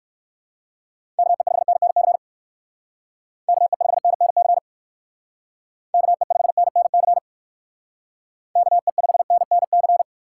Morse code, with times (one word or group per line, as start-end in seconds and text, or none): 1.2-2.2 KE5NNC
3.5-4.6 KE5NNC
5.9-7.2 KE5NNC
8.6-10.0 KE5NNC